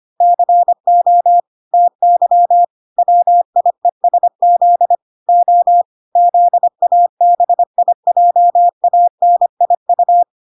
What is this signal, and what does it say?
Morse code, heard as a keyed tone, about 25 wpm, 700 Hz. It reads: CO TY WIESZ O ZABIJANIU